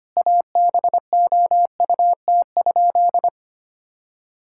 Morse code, heard as a keyed tone, about 25 wpm, 700 Hz. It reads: ABOUT?
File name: practice / 25wpm / Sentences with 2-Word Phrases - 25wpm